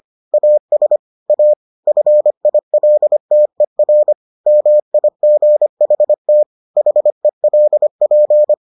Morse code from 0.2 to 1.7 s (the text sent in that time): AS A